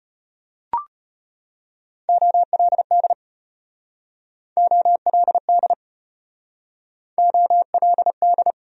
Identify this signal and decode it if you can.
Morse code, heard as a keyed tone, about 33 wpm, 700 Hz. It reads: OLD OLD OLD